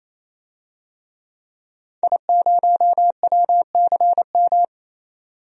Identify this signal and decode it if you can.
Morse code, heard as a keyed tone, about 28 wpm, 700 Hz. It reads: I0WCM
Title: DXCC-100 Call Signs - 5 Characters - 28wpm